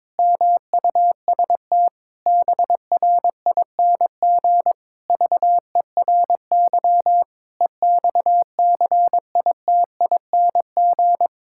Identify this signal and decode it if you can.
Morse code, heard as a keyed tone, about 22 wpm, 700 Hz. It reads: MUST BRING VERY EXCITING
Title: Pangram Challenge v2 - 22wpm